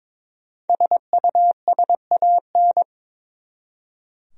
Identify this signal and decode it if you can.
Morse code, heard as a keyed tone, about 22 wpm, 700 Hz. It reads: SUSAN
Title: USA Common Names - Max 5 Letters - 22wpm